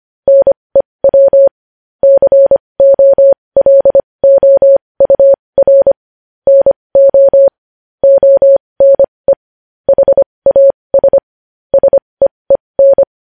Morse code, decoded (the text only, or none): NEW COLOUR NO ONE HAS SEEN